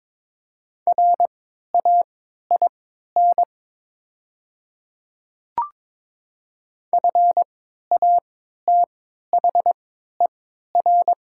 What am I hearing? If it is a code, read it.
Morse code, heard as a keyed tone, about 22 wpm, spaced as for 13 wpm, 690 Hz. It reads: RAIN FATHER
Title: 3x Character Spacing - Top 300 Words - 22wpm